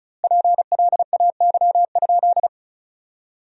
Morse code, heard as a keyed tone, about 34 wpm, 700 Hz. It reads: PLAY?
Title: 2x Word-Spacing - Sentences from Top 400 Words - 35wpm